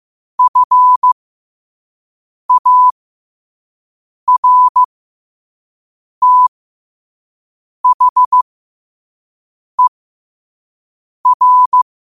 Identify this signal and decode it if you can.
Morse code, heard as a keyed tone, about 15 wpm, spaced as for 5 wpm, 1000 Hz. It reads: FARTHER
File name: qotdfiles / 5wpm